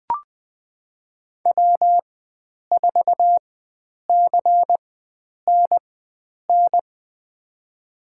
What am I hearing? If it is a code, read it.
Morse code, heard as a keyed tone, about 21 wpm, spaced as for 9 wpm, 700 Hz. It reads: W4CNN
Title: USA Call Signs - 5 Characters - 4x Character Spacing - 20wpm